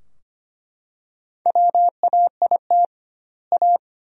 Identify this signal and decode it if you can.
Morse code, heard as a keyed tone, about 25 wpm, 700 Hz. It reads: WAIT A
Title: 2x Word-Spacing - Sentences from Top 500 Words - 25wpm